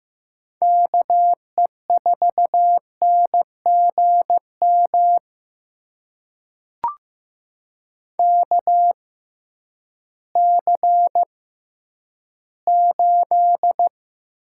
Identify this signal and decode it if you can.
Morse code, heard as a keyed tone, about 15 wpm, spaced as for 10 wpm, 700 Hz. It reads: KE4NGM KC8
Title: USA Call Signs - 6 Characters - 6x Character Spacing - 15wpm